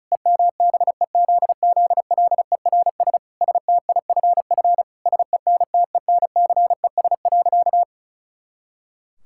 Morse code, heard as a keyed tone, about 34 wpm, 700 Hz. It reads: EMBEZZLERS STIFF SENTENCES.
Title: Pangram Challenge v2 - 35wpm